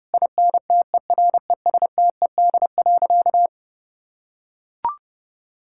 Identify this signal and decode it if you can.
Morse code, heard as a keyed tone, about 32 wpm, 700 Hz. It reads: INTERESTED.
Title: Sentences from Top 1000 Words - Review Part 6 - 30wpm